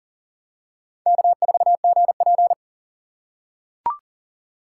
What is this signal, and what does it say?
Morse code, heard as a keyed tone, about 40 wpm, 700 Hz. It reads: K4GP
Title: USA Call Signs - 4 Characters - 3x Character Spacing - 40wpm